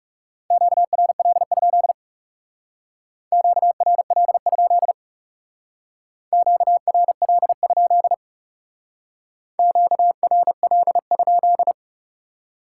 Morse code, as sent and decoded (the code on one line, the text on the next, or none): --.- .-. .-.. ..--.. / --.- .-. .-.. ..--.. / --.- .-. .-.. ..--.. / --.- .-. .-.. ..--..
QRL? QRL? QRL? QRL?